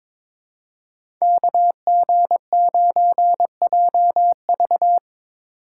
Morse code, heard as a keyed tone, about 22 wpm, 700 Hz. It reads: KG9JV
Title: USA Call Signs - 5 Characters - 8x Character Spacing - 22wpm